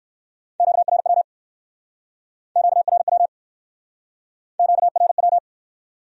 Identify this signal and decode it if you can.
Morse code, heard as a keyed tone, about 64 wpm, 700 Hz. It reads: QRW QRW QRW